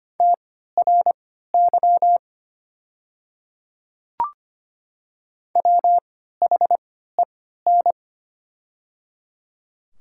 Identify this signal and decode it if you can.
Morse code, heard as a keyed tone, about 25 wpm, spaced as for 14 wpm, 700 Hz. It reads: TRY WHEN
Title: 3x Character Spacing - Top 200 Words - 25wpm